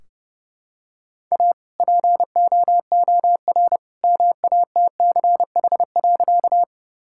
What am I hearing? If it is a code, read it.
Morse code, heard as a keyed tone, about 30 wpm, 700 Hz. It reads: A POOR MATCH.
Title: Sentences from Top 1600 Words 30wpm